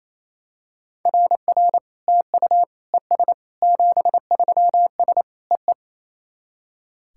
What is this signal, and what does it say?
Morse code, heard as a keyed tone, about 28 wpm, 700 Hz. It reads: RR TU ES 73S EE
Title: SOTA QSO Practice 28wpm